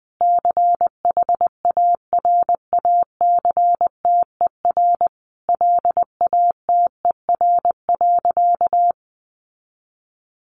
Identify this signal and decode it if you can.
Morse code, heard as a keyed tone, about 20 wpm, 700 Hz. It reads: CHARACTER LATER.
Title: Sentences from Top 1200 Words 20wpm